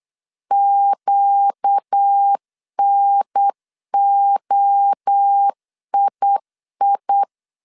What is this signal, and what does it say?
Morse code, heard as a keyed tone, about 8 wpm, 800 Hz. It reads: QNOII
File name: CW / DET / 1 6 1500C S B 8wpm 5142